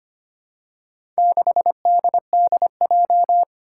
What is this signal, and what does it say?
Morse code, heard as a keyed tone, about 25 wpm, 700 Hz. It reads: 6DDJ
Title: Four Challenging Letter-Number Combinations 25wpm